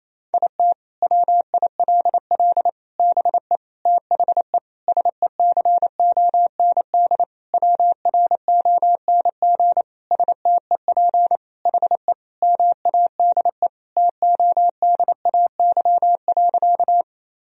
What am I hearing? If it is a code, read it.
Morse code, heard as a keyed tone, about 28 wpm, 700 Hz. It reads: IT WILL BE THE SECOND WRONG STEP HE MADE TODAY.
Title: Sentences from Top 1000 Words - Review Part 2 - 28wpm